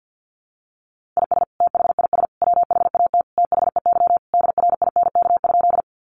Morse code, heard as a keyed tone, about 50 wpm, 700 Hz. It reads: IS THIS WHAT THEY DRINK?
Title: Sentences from Top 1500 Words 50wpm